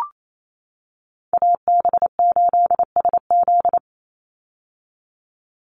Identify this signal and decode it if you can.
Morse code, heard as a keyed tone, about 28 wpm, 700 Hz. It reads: AB8SZ
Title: USA Call Signs 28wpm